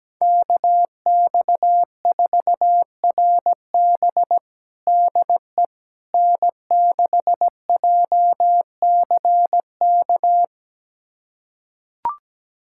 Morse code, heard as a keyed tone, about 17 wpm, 700 Hz. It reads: KX4RB DE N6JCK
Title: USA Callsign DE Callsign - 5 Characters - 17wpm